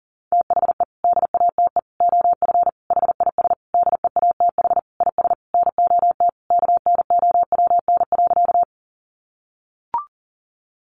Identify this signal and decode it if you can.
Morse code, heard as a keyed tone, about 40 wpm, 700 Hz. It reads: THE DATE OF HIS DEATH IS NOT KNOWN.
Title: Sentences from Top 1100 Words 40wpm